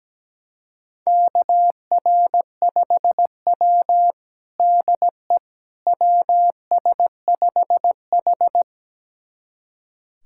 Morse code, heard as a keyed tone, about 17 wpm, 700 Hz. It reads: KR5W DE WS5H
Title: USA Callsign DE Callsign - 4 Characters - 17wpm